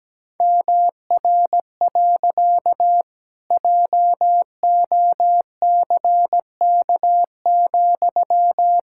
Morse code, heard as a keyed tone, about 17 wpm, 700 Hz. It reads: MR. JOCK,
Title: Pangram Challenge v2 - 17wpm